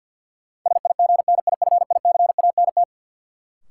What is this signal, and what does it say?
Morse code, heard as a keyed tone, about 53 wpm, 700 Hz. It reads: SIGNIFICANT
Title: Sentences from Top 1600 Words - New Word Drill Alt - 50wpm